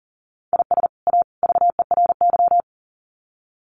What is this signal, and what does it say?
Morse code, heard as a keyed tone, about 40 wpm, 700 Hz. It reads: IS A VERY
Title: Sets of 3 Words 40wpm